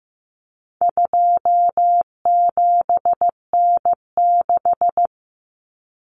Morse code, heard as a keyed tone, about 15 wpm, 700 Hz. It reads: 27N6
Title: Common Cut Numbers - 4-Digit 15wpm